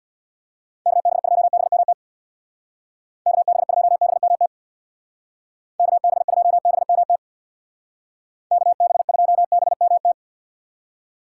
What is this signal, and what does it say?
Morse code, heard as a keyed tone, about 64 wpm, 700 Hz. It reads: KB2BMT KB2BMT KB2BMT KB2BMT